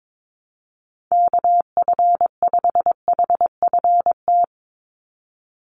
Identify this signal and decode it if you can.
Morse code, heard as a keyed tone, about 22 wpm, 700 Hz. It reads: KF5HFT
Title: USA Call Signs 22wpm